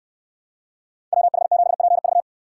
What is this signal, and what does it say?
Morse code, heard as a keyed tone, about 69 wpm, 700 Hz. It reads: WH7PV